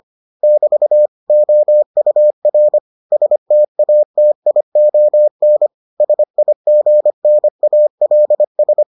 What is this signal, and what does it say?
Morse code, heard as a keyed tone, about 25 wpm, 600 Hz. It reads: <BT> OUR STATION SIGNALS